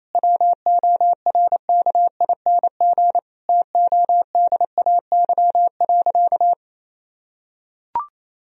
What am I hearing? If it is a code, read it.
Morse code, heard as a keyed tone, about 27 wpm, 700 Hz. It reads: WORKING TODAY.
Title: Sentences from Top 1000 Words - Review Part 4 - 28wpm